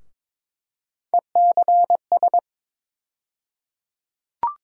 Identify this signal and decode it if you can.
Morse code, heard as a keyed tone, about 22 wpm, 700 Hz. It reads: ECS
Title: USA Call Sign Postfixes - 3 Letters 22wpm